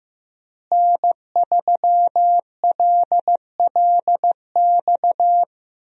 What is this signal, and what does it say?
Morse code, heard as a keyed tone, about 15 wpm, 700 Hz. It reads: N3LLX
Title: USA Call Signs - 5 Characters - Repeated 3x - 15wpm